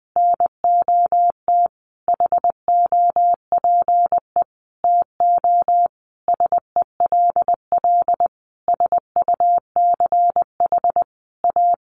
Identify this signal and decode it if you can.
Morse code, heard as a keyed tone, about 20 wpm, 700 Hz. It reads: NOT HOPE TO SELL SUCH A
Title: Sentences from Top 1100 Words 20wpm